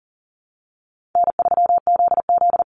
Morse code, heard as a keyed tone, about 40 wpm, 700 Hz. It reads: N3ZZ
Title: Call Signs Repeated 3x - 40wpm